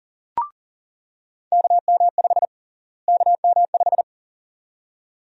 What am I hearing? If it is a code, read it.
Morse code, heard as a keyed tone, about 40 wpm, 700 Hz. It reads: KM5 KM5
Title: USA Call Sign Prefixes - Repeated 2x - 40wpm